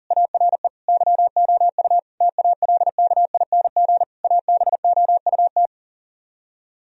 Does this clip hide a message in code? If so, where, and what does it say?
Yes: Morse code from 0.1 to 5.7 s, heard as a keyed tone, about 51 wpm, 700 Hz. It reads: ARE YOU TALKING ABOUT